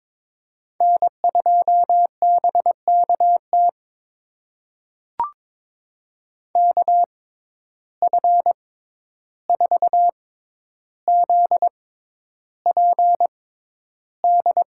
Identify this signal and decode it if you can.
Morse code, heard as a keyed tone, about 21 wpm, spaced as for 11 wpm, 700 Hz. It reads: N2BKT KF4ZPD
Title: USA Call Signs - 6x Character Spacing - 22wpm